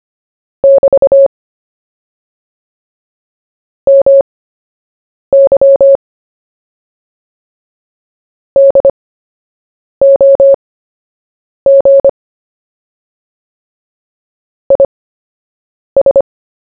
Morse code, heard as a keyed tone, about 25 wpm, spaced as for 7 wpm, 550 Hz. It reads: <BT> MY DOG IS
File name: cw / fists / Short-Story-1-7wpm